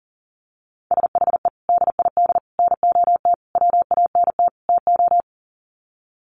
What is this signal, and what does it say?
Morse code, heard as a keyed tone, about 40 wpm, 700 Hz. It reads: SHE DID NOT WANT TO
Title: Sets of 5 Words 40wpm